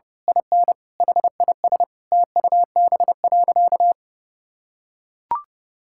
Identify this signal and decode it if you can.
Morse code, heard as a keyed tone, about 30 wpm, 700 Hz. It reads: IN HIS TUB.
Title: Pangram Challenge v2 - 30wpm